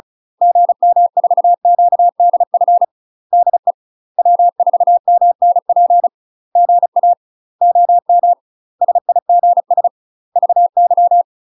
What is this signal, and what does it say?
Morse code, heard as a keyed tone, about 34 wpm, 700 Hz. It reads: GM4QDF DE W4MNP GA OM SIGS VY
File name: UK MASTER QSOs TRAINING / 35WPM / UK MASTER QSOs 0034